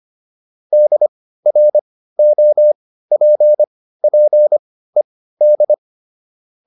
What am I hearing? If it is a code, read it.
Morse code, heard as a keyed tone, about 25 wpm, spaced as for 15 wpm, 600 Hz. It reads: DROPPED